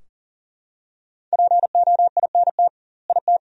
Morse code, heard as a keyed tone, about 37 wpm, 700 Hz. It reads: POINT IT
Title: 2x Word-Spacing - Sentences from Top 300 Words - 40wpm